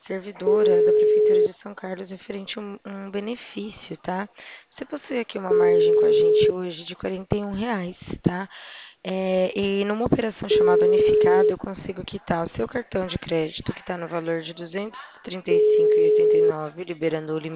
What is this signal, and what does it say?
Ringback tone (Europe, 425 Hz), with speech.